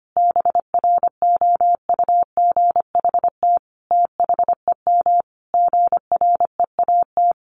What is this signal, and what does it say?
Morse code, heard as a keyed tone, about 25 wpm, 700 Hz. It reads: BROUGHT THEM GREAT
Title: Sentences from Top 1300 Words 25wpm